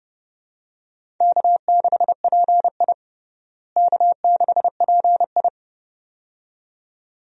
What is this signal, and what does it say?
Morse code, heard as a keyed tone, about 30 wpm, 700 Hz. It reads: K6PI K6PI
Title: USA Call Signs - 4 Characters - Repeated 2x - 30wpm